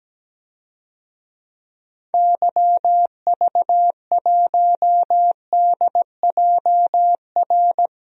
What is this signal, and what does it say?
Morse code, heard as a keyed tone, about 17 wpm, 700 Hz. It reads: YV1DJR
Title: DXCC-100 Call Signs - 6 Characters - 17wpm